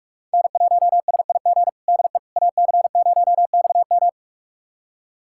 Morse code, heard as a keyed tone, about 46 wpm, 700 Hz. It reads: N1SIG DE AC0XM
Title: USA Callsign DE Callsign - 5 Characters - 45wpm